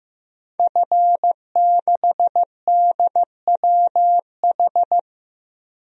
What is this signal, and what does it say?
Morse code, heard as a keyed tone, about 16 wpm, 700 Hz. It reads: F6DWH
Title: DXCC-100 Call Signs - 5 Characters - Repeated 2x - 15wpm